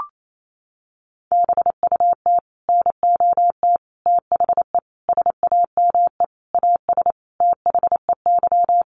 Morse code, heard as a keyed tone, about 28 wpm, 700 Hz. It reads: BUT NOT THE SAME AS THEY
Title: A Sound Of Thunder 28wpm